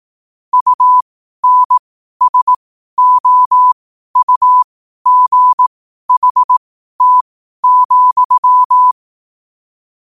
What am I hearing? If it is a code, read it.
Morse code, heard as a keyed tone, about 18 wpm, spaced as for 12 wpm, 1000 Hz. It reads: UNSOUGHT,
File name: qotdfiles / 12wpm